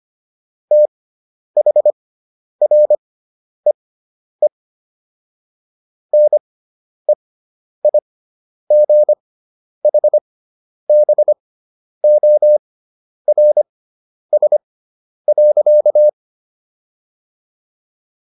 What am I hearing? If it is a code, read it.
Morse code, heard as a keyed tone, about 25 wpm, spaced as for 10 wpm, 600 Hz. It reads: THREE NEIGHBORS.